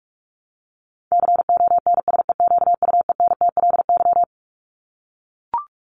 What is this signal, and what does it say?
Morse code, heard as a keyed tone, about 44 wpm, 700 Hz. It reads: CONSEQUENTLY
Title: LY Suffix - Top 100 Words with Base Words 45wpm